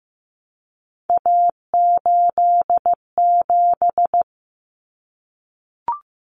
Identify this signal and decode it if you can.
Morse code, heard as a keyed tone, about 15 wpm, 700 Hz. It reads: A87
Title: Common Cut Numbers - 3-Digit 15wpm